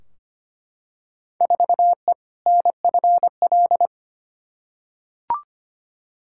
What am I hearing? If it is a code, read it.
Morse code, heard as a keyed tone, about 25 wpm, 700 Hz. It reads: 4E NFL
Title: Field Day Exchanges Encore - 25wpm